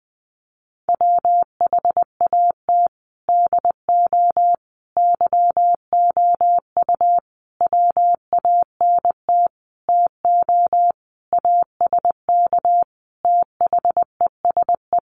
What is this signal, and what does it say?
Morse code, heard as a keyed tone, about 20 wpm, 700 Hz. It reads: WHAT DO YOU WANT TO ASK THESE
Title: Sentences from Top 200 Words 20wpm